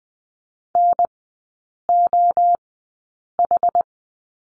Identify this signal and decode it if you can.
Morse code, heard as a keyed tone, about 20 wpm, spaced as for 8 wpm, 700 Hz. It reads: NOH
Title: 218 Lesson 35e - Number 0 - Three Characters 20wpm